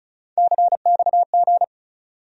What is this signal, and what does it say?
Morse code, heard as a keyed tone, about 36 wpm, 700 Hz. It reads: CXG
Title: USA Call Sign Postfixes - 3 Letters 35wpm